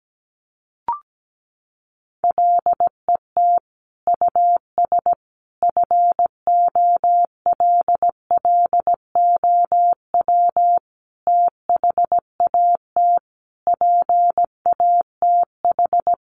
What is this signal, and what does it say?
Morse code, heard as a keyed tone, about 17 wpm, 700 Hz. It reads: LET US FOLLOW THAT PATH
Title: Sentences from Top 1400 Words 17wpm